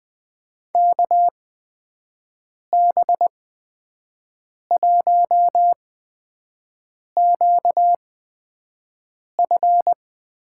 Morse code, heard as a keyed tone, about 20 wpm, spaced as for 5 wpm, 700 Hz. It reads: KB1QF
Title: USA Call Signs - 6 Characters - 8x Character Spacing - 20wpm